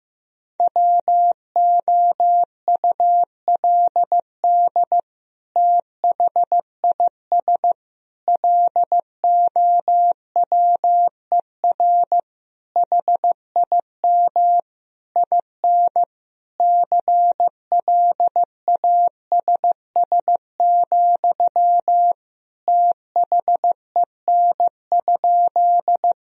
Morse code, heard as a keyed tone, about 15 wpm, 700 Hz. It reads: WOULD THIS LOWER HIM IN CLASS, THEN?
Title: Sentences from Top 1000 Words - Review Part 4 - 15wpm